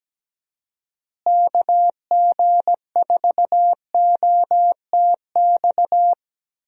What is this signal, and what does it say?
Morse code, heard as a keyed tone, about 17 wpm, 700 Hz. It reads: KG4OTX
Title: USA Call Signs - 6 Characters - Repeated 3x - 17wpm